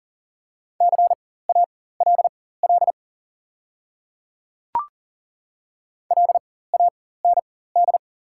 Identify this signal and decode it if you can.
Morse code, heard as a keyed tone, about 40 wpm, spaced as for 19 wpm, 700 Hz. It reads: CALL LAND
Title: 4x Character Spacing - Top 300 Words - 40wpm